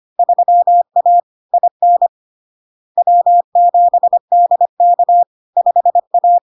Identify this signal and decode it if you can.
Morse code, heard as a keyed tone, about 25 wpm, 700 Hz. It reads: 3A IN W7DK 5A
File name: LARK-FD-2022-Log-in-CW-25wpm-700-Hz